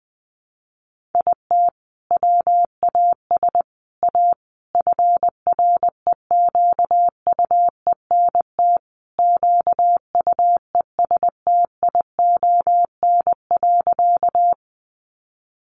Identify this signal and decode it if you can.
Morse code, heard as a keyed tone, about 20 wpm, 700 Hz. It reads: IT WAS A FREQUENT QUESTION.
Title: Sentences from Top 1400 Words 20wpm